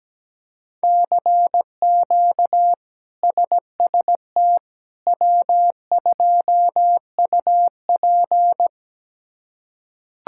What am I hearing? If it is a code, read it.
Morse code, heard as a keyed tone, about 17 wpm, 700 Hz. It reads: CQ SST W2UP
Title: K1USN SST QSO Practice - 17wpm